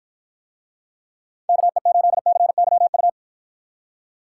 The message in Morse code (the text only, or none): KE8CYU